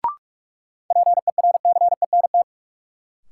Morse code, heard as a keyed tone, about 41 wpm, 700 Hz. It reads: PERCENT
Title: Sets of 2 Words - New Word Drill - 45wpm